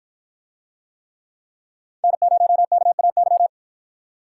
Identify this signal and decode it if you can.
Morse code, heard as a keyed tone, about 49 wpm, 700 Hz. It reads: N0KAY